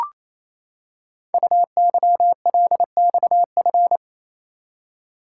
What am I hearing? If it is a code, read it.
Morse code, heard as a keyed tone, about 29 wpm, 700 Hz. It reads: UYLXF
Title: Code Groups - 5 Letters - 28wpm